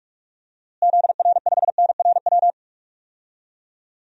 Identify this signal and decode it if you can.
Morse code, heard as a keyed tone, about 46 wpm, 700 Hz. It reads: ZR5NRW
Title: DXCC-100 Call Signs - 6 Characters - Repeated 3x - 45wpm